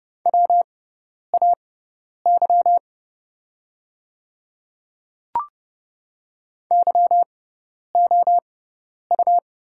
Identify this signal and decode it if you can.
Morse code, heard as a keyed tone, about 29 wpm, spaced as for 10 wpm, 700 Hz. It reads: WAY YOU
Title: 6x Character Spacing - Top 300 Words - 30wpm